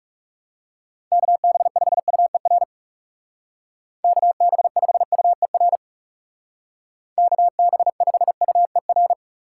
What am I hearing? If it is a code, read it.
Morse code, heard as a keyed tone, about 37 wpm, 700 Hz. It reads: KB5UER KB5UER KB5UER
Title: Speed Racing - USA 2x3 Call Signs 30wpm